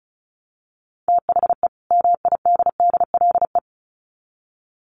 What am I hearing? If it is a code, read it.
Morse code, heard as a keyed tone, about 34 wpm, 700 Hz. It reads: THE MIDDLE
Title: Sets of 2 Words 35wpm